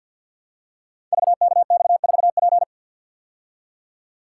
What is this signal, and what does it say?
Morse code, heard as a keyed tone, about 53 wpm, 700 Hz. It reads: VKX4P
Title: Code Groups - 5 Letter:Numbers - 50wpm